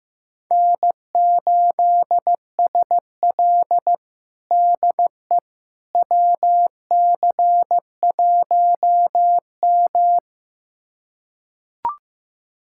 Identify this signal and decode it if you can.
Morse code, heard as a keyed tone, about 15 wpm, 700 Hz. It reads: N8SL DE WC1M
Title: USA Callsign DE Callsign - 4 Characters - 15wpm